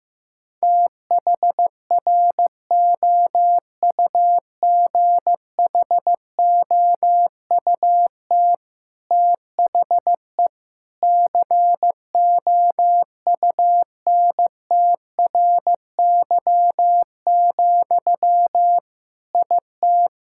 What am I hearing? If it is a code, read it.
Morse code, heard as a keyed tone, about 15 wpm, 700 Hz. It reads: THROUGHOUT THE COUNTRY, IT
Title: Sentences from Top 1000 Words - Review Part 5 - 15wpm